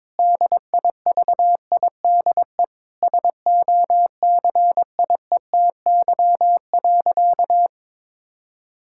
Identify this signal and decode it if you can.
Morse code, heard as a keyed tone, about 22 wpm, 700 Hz. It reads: DIVIDE SOCIETY.
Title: Sentences from Top 1600 Words 22wpm